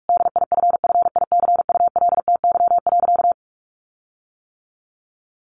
Morse code, heard as a keyed tone, about 44 wpm, 700 Hz. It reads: DIFFICULTY.
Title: Sentences from Top 1500 Words 45wpm